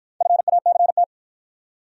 Morse code, heard as a keyed tone, about 53 wpm, 700 Hz. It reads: FACT